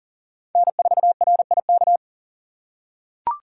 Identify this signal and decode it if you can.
Morse code, heard as a keyed tone, about 40 wpm, 700 Hz. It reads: N4RIK
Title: USA Call Signs - 5 Characters - 2x Character Spacing - 40wpm